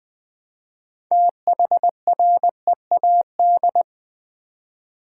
Morse code, heard as a keyed tone, about 20 wpm, 700 Hz. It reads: THREAD
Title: Build A Word - 3 Letters At A Time - 6-Letter Words - 20wpm